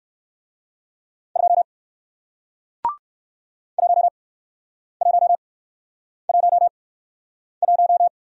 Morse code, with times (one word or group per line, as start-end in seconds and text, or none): 1.4-1.6 2
3.8-8.1 1111